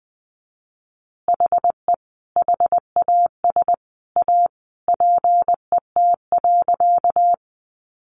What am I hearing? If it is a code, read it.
Morse code, heard as a keyed tone, about 20 wpm, 700 Hz. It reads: HE HAS A PET.
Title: Easy Phrases 20wpm